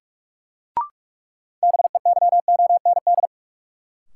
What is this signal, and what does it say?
Morse code, heard as a keyed tone, about 46 wpm, 700 Hz. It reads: BEYOND
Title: Sentences from Top 1000 Words - Review Part 6 - New Word Drill - 45wpm